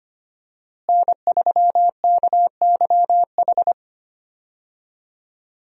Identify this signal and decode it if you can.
Morse code, heard as a keyed tone, about 25 wpm, 700 Hz. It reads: N3KYH
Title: USA Call Signs - 5 Characters - Repeated 2x - 25wpm